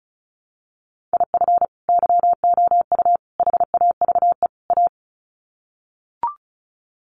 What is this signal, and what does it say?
Morse code, heard as a keyed tone, about 34 wpm, 700 Hz. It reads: IF YOU HAVE A
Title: Sets of 4 Words 35wpm